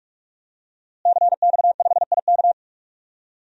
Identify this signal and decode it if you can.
Morse code, heard as a keyed tone, about 46 wpm, 700 Hz. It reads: CX5IK
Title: DXCC-100 Call Signs - 5 Characters - Repeated 2x - 45wpm